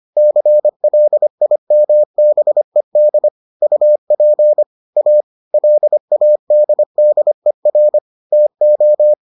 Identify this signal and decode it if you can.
Morse code, heard as a keyed tone, about 25 wpm, 600 Hz. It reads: CLIMBED UP A LADDER TO